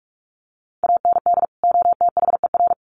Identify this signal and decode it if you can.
Morse code, heard as a keyed tone, about 44 wpm, 700 Hz. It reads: AND OTHER
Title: Sets of 2 Words 45wpm